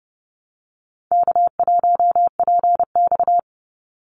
Morse code, heard as a keyed tone, about 30 wpm, 700 Hz. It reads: K1PX
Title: Easy Call Signs Repeated 2x - 30wpm